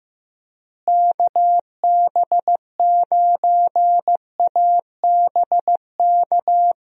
Morse code, heard as a keyed tone, about 15 wpm, 700 Hz. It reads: KB9ABK
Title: USA Call Signs - 6 Characters - Repeated 3x - 15wpm